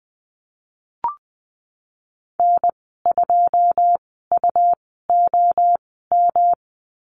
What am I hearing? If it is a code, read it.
Morse code, heard as a keyed tone, about 20 wpm, spaced as for 15 wpm, 700 Hz. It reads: N2UOM
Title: 161 Lesson 26g - Letter g - Call Signs 20wpm